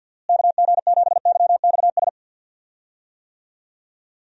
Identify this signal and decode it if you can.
Morse code, heard as a keyed tone, about 51 wpm, 700 Hz. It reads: KG7YXS
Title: USA Call Signs - 6 Characters - Repeated 3x - 50wpm